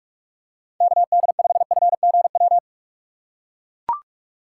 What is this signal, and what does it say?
Morse code, heard as a keyed tone, about 42 wpm, 700 Hz. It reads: KD5FGW
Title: USA Call Signs - 4x Character Spacing - 45wpm